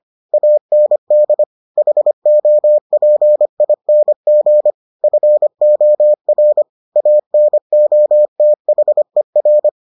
Morse code, heard as a keyed tone, about 25 wpm, 600 Hz. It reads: AND HOPING FOR ANOTHER